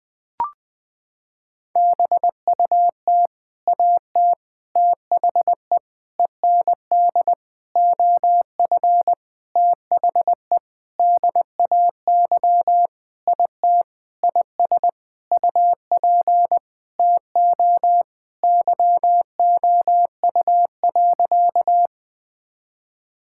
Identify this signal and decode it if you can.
Morse code, heard as a keyed tone, about 20 wpm, 700 Hz. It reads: BUT AT THE END OF THE DAY IT IS UP TO YOU.
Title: Sentences with 7-Word Phrases - 20wpm